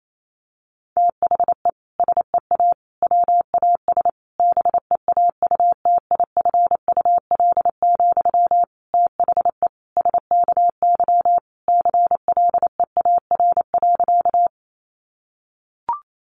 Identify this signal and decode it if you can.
Morse code, heard as a keyed tone, about 28 wpm, 700 Hz. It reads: THE SEA WAS BEAUTIFUL, THE SKY CLEAR.
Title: Sentences from Top 1200 Words 28wpm